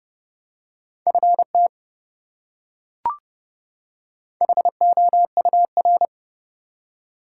Morse code, heard as a keyed tone, about 30 wpm, 700 Hz. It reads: FT HOUR
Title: Top 180 QSO Elements 30wpm